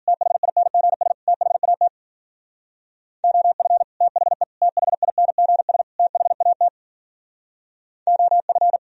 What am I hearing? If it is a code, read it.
Morse code, heard as a keyed tone, about 49 wpm, 700 Hz. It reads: THINGS THAT OF THE THINGS THAT OF
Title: Speed-Racing - Sets of 4 Words 40wpm